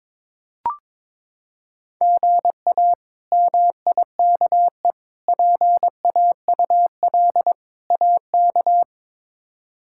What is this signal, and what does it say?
Morse code, heard as a keyed tone, about 22 wpm, 700 Hz. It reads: GA MIKE PAUL AK
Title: K1USN SST QSO Practice - 22wpm